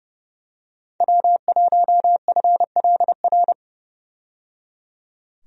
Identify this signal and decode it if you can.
Morse code, heard as a keyed tone, about 31 wpm, 700 Hz. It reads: W1FLR
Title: USA Call Signs - 5 Characters - 30wpm